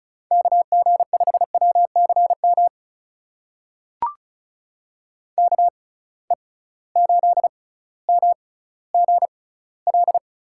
Morse code, heard as a keyed tone, about 36 wpm, spaced as for 18 wpm, 700 Hz. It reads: KG5WCM KE8MGL
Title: USA Call Signs - 6 Characters - 6x Character Spacing - 35wpm